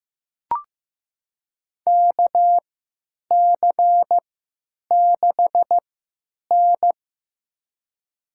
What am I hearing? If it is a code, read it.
Morse code, heard as a keyed tone, about 15 wpm, spaced as for 9 wpm, 700 Hz. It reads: KC6N
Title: USA Call Signs - 4 Characters - 3x Character Spacing - 15wpm